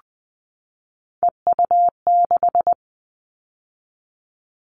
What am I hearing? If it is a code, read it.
Morse code, heard as a keyed tone, about 20 wpm, 700 Hz. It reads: EU6
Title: Callsign Prefixes for Top 50 DXCC - 20wpm